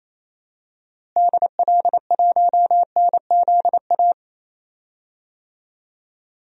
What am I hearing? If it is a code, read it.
Morse code, heard as a keyed tone, about 28 wpm, 700 Hz. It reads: DL1NZA
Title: DXCC-100 Call Signs - 6 Characters - Repeated 3x - 28wpm